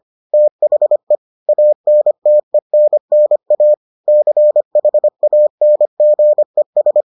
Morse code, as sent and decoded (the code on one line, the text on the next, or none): - .... / .- -. - . -. -. .- / -.-. .... .- -. --. ...
THE ANTENNA CHANGES